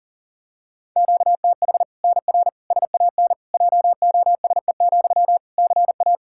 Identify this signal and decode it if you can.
Morse code, heard as a keyed tone, about 37 wpm, 700 Hz. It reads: QTH NR SAN JOSE, CA
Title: Practice QSO - Exchange 1 - Beginning - 40wpm